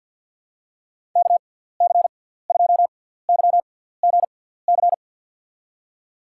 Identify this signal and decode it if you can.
Morse code, heard as a keyed tone, about 47 wpm, spaced as for 17 wpm, 700 Hz. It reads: KC2YGC
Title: USA Call Signs - 6 Characters - 6x Character Spacing - 50wpm